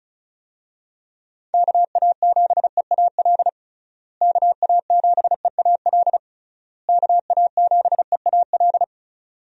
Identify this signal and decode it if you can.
Morse code, heard as a keyed tone, about 36 wpm, 700 Hz. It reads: KA7EAL KA7EAL KA7EAL
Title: USA Call Signs - 6 Characters - Repeated 3x - 35wpm